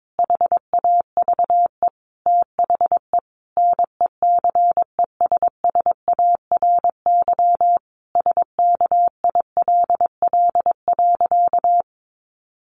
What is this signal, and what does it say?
Morse code, heard as a keyed tone, about 22 wpm, 700 Hz. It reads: HAVE THE NECESSARY SKILL.
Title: Sentences from Top 1200 Words 22wpm